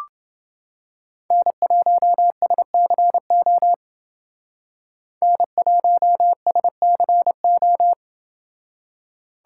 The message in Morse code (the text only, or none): N1SCO N1SCO